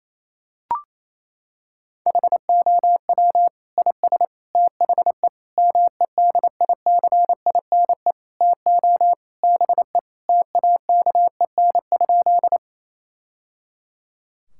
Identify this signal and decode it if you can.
Morse code, heard as a keyed tone, about 28 wpm, 700 Hz. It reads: HOW IS THE MEDICINE TO BE TAKEN?
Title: Sentences from Top 1000 Words - Review Part 6 - 28wpm